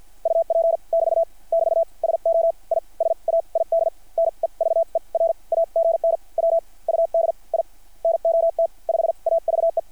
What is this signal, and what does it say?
Morse code, heard as a keyed tone, about 50 wpm, spaced as for 39 wpm, 660 Hz. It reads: UP <BT> <BT> SO I SAID NEVER AGN WUD I NOT 5AVE
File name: archived / CRSnet